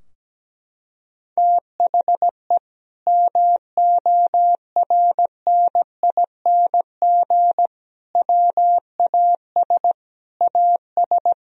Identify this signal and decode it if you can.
Morse code, heard as a keyed tone, about 17 wpm, 700 Hz. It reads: THE MORNING WAS AS
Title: Sentences from Top 1000 Words - Review Part 3 - 17wpm